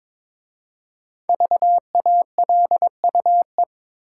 Morse code, heard as a keyed tone, about 22 wpm, 700 Hz. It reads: VALUE
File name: 5 Letter Words - First 3 Letters Given - 22wpm